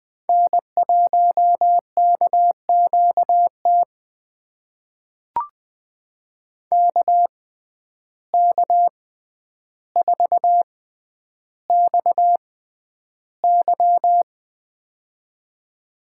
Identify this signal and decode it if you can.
Morse code, heard as a keyed tone, about 21 wpm, spaced as for 10 wpm, 700 Hz. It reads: N1KQT KK4XY
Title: USA Call Signs - 5 Characters - 6x Character Spacing - 20wpm